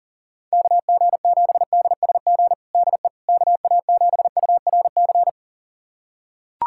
Morse code, heard as a keyed tone, about 40 wpm, 690 Hz. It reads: KG7DSG DE KA7URC
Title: USA Callsign DE Callsign - 6 Characters - 40wpm